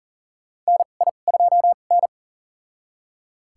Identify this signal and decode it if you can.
Morse code, heard as a keyed tone, about 40 wpm, spaced as for 29 wpm, 700 Hz. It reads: NI2N
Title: USA Call Signs - 2x Character Spacing - 40wpm